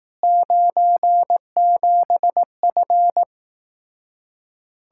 Morse code, heard as a keyed tone, about 18 wpm, 700 Hz. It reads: 97F
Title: Speed-Racing Common QSO Phrases Encore - 15wpm